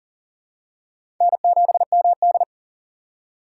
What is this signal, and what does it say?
Morse code, heard as a keyed tone, about 40 wpm, 700 Hz. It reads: N7MD
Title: USA Call Signs - 4 Characters - 3x Character Spacing - 40wpm